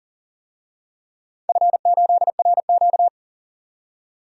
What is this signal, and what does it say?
Morse code, heard as a keyed tone, about 40 wpm, 700 Hz. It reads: F8RQ
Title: DXCC-100 Call Signs - 4 Characters - 40wpm